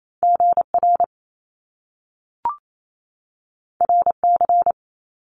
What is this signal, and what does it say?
Morse code, heard as a keyed tone, about 28 wpm, 700 Hz. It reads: GR RC
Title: Top Two Letter Combinations in Words 28wpm